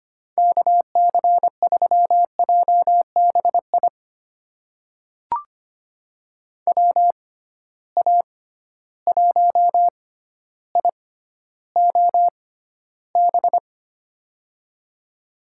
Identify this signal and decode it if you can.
Morse code, heard as a keyed tone, about 25 wpm, spaced as for 13 wpm, 700 Hz. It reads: KC3JBI WA1IOB
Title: USA Call Signs - 6 Characters - 6x Character Spacing - 25wpm